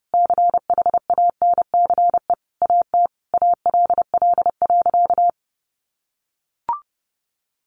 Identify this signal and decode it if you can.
Morse code, heard as a keyed tone, about 30 wpm, 700 Hz. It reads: CHANCE AT ALL.
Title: Sentences from Top 800 Words 30wpm